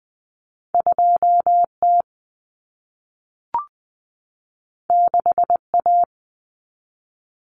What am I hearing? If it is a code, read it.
Morse code, heard as a keyed tone, about 20 wpm, 700 Hz. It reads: 2T 6A
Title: Common Cut Numbers - 2-Digit 20wpm